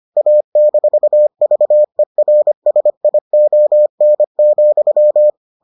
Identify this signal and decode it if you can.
Morse code, heard as a keyed tone, about 25 wpm, 600 Hz. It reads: A-VERSION,